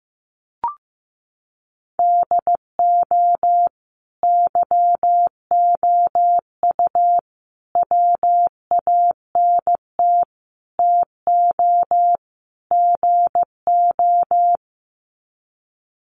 Morse code, heard as a keyed tone, about 15 wpm, 700 Hz. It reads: DO YOU WANT TO GO
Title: Sets of 5 Words Encore2 15wpm